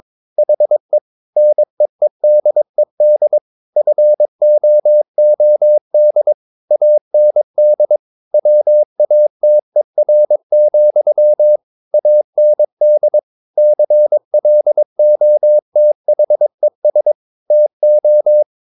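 Morse code, heard as a keyed tone, about 22 wpm, 600 Hz. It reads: HE NEEDED FOOD AND WATER, AND CLOTHES TO
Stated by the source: Morse code practice at 22 wpm, sending text from an English novel.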